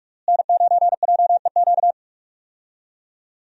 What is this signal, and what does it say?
Morse code, heard as a keyed tone, about 46 wpm, 700 Hz. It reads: N9JEQ
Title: USA Call Signs - 5 Characters - Repeated 3x - 45wpm